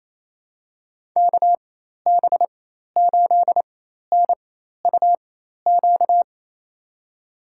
Morse code, heard as a keyed tone, about 27 wpm, spaced as for 13 wpm, 700 Hz. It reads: KB8NUQ
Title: USA Call Signs - 6 Characters - 4x Character Spacing - 28wpm